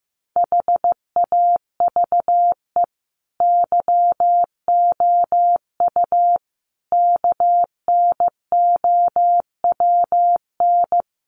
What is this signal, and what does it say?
Morse code, heard as a keyed tone, about 14 wpm, 700 Hz. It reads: HAVE YOU KNOWN
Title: Sentences from Top 1300 Words 15wpm